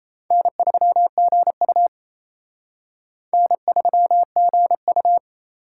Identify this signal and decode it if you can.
Morse code, heard as a keyed tone, about 34 wpm, 700 Hz. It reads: N3GU N3GU
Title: Speed Racing - USA 1x2 Call Signs 28wpm